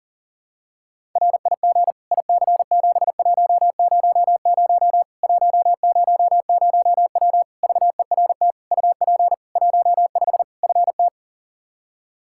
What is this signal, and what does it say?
Morse code, heard as a keyed tone, about 40 wpm, 700 Hz. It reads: RIG IC7100 100W VERT UP 15 FT
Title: K1USN SST QSO Practice 40wpm